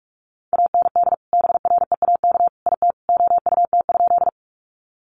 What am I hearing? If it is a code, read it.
Morse code, heard as a keyed tone, about 44 wpm, 700 Hz. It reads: AND BREAK IT OUT?
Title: Sentences from Top 900 Words 45wpm